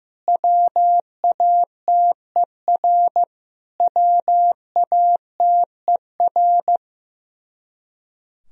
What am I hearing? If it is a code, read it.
Morse code, heard as a keyed tone, about 15 wpm, 700 Hz. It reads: WATER WATER
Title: Top 100 Words Repeated 15wpm